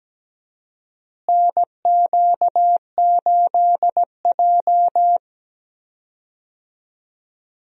Morse code, heard as a keyed tone, about 17 wpm, 700 Hz. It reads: NQ8J
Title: USA Call Signs - 4 Characters - Repeated 2x - 17wpm